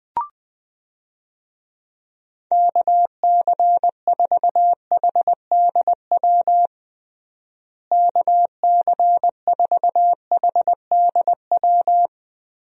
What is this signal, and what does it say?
Morse code, heard as a keyed tone, about 20 wpm, 700 Hz. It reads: KC4HDW KC4HDW
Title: USA Call Signs - 6 Characters - Repeated 2x - 20wpm